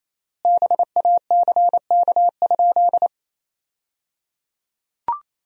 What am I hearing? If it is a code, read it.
Morse code, heard as a keyed tone, about 28 wpm, 700 Hz. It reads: BACK?
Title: 3x Word-Spacing - Sentences from Top 400 Words - 28wpm